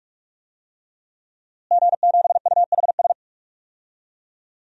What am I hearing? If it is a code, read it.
Morse code, heard as a keyed tone, about 41 wpm, 700 Hz. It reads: G7UHS
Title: DXCC-100 Call Signs - 5 Characters - Repeated 3x - 45wpm